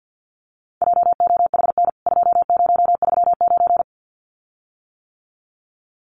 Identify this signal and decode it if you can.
Morse code, heard as a keyed tone, about 50 wpm, 700 Hz. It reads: JOHN 2039
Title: CWT Exchange 50wpm